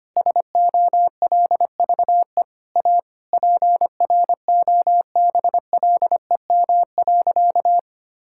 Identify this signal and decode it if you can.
Morse code, heard as a keyed tone, about 25 wpm, 700 Hz. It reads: SOLVE A PROBLEM.
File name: practice / 25wpm / Sentences with 9-Word Phrases - 25wpm